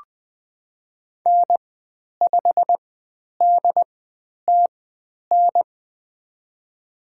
Morse code, heard as a keyed tone, about 20 wpm, spaced as for 10 wpm, 700 Hz. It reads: N5DTN
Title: 102 Lesson 17g - Letter u - Call Signs 20wpm